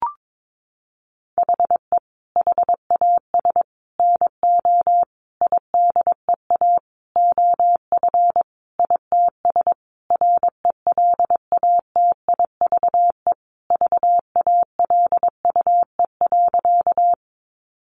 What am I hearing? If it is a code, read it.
Morse code, heard as a keyed tone, about 22 wpm, 700 Hz. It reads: HE HAS NO IDEA OF ITS RELATIVE VALUE.
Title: Sentences from Top 1400 Words 22wpm